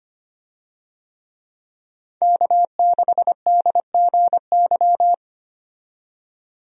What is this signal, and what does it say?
Morse code, heard as a keyed tone, about 25 wpm, 700 Hz. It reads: K6DGY
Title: USA Call Signs - 5 Characters - 25wpm